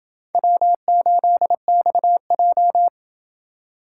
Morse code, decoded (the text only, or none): W8XJ